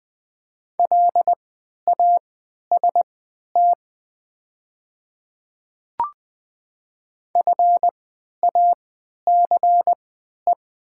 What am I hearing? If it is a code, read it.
Morse code, heard as a keyed tone, about 20 wpm, spaced as for 11 wpm, 700 Hz. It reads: LAST FACE
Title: 3x Character Spacing - Top 300 Words - 20wpm